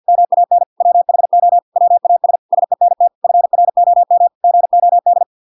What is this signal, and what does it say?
Morse code, heard as a keyed tone, about 50 wpm, 700 Hz. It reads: MAN WHO WAS SENT FROM GOD